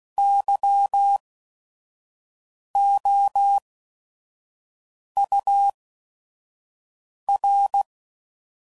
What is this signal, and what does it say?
Morse code, heard as a keyed tone, about 16 wpm, spaced as for 5 wpm, 800 Hz. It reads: YOUR
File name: CW Code Groups @5WPM / RST's / Your RST's